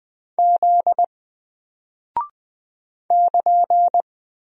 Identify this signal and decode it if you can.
Morse code, heard as a keyed tone, about 20 wpm, 700 Hz. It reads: Z <KN>
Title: Single Character (Letter:Number:Punctuation:Prosign) Alt 20wpm